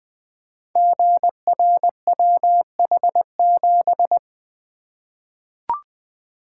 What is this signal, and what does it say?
Morse code, heard as a keyed tone, about 20 wpm, 700 Hz. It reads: GRWH7
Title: Code Groups - 5 Letter:Numbers - 20wpm